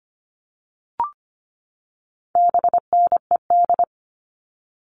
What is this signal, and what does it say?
Morse code, heard as a keyed tone, about 25 wpm, 700 Hz. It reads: BNED
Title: All Cut Numbers - 4-Digit 25wpm